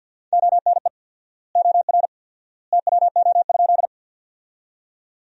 Morse code, heard as a keyed tone, about 50 wpm, 700 Hz. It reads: ONE OR TWO?
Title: 4x Word-Spacing - Sentences from Top 100 Words - 50wpm